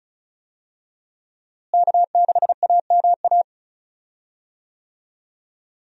Morse code, heard as a keyed tone, about 36 wpm, 700 Hz. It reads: K6AMA
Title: USA Call Signs - 5 Characters - Repeated 2x - 35wpm